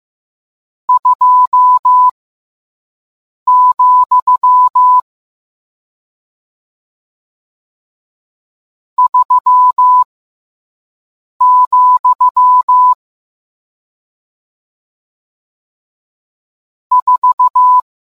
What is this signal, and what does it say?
Morse code, heard as a keyed tone, about 15 wpm, spaced as for 5 wpm, 1000 Hz. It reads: 2, 3, 4